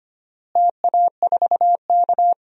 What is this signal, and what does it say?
Morse code, heard as a keyed tone, about 25 wpm, 700 Hz. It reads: TA4K
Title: DXCC-100 Call Signs - 4 Characters - Repeated 2x - 25wpm